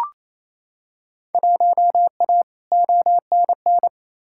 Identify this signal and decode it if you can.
Morse code, heard as a keyed tone, about 28 wpm, 700 Hz. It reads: 1A ONN
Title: Field Day Exchanges Encore - 28wpm